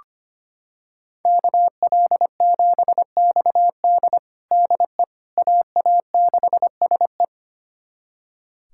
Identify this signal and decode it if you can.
Morse code, heard as a keyed tone, about 25 wpm, 700 Hz. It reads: KL7XD DE AA6SE
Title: USA Callsign DE Callsign - 5 Characters - 25wpm